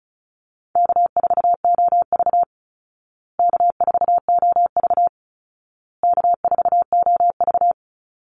Morse code, heard as a keyed tone, about 34 wpm, 700 Hz. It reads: K4OV K4OV K4OV
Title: Easy Call Signs Repeated 3x - 35wpm